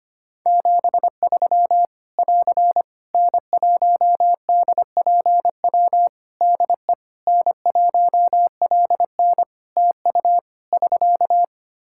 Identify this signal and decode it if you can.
Morse code, heard as a keyed tone, about 25 wpm, 700 Hz. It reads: 73 <AR> N1DPW DE N1LN TU <SK>
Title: Practice QSO - Exchange 4 - Ending - 25wpm